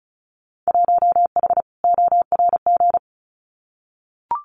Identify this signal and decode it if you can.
Morse code, heard as a keyed tone, about 34 wpm, 700 Hz. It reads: 1H ORG
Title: Winter Field Day Exchanges 35wpm